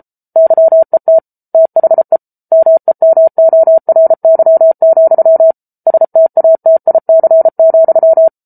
Morse code, heard as a keyed tone, about 34 wpm, 650 Hz. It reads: YET THE MEMORY, STATIC,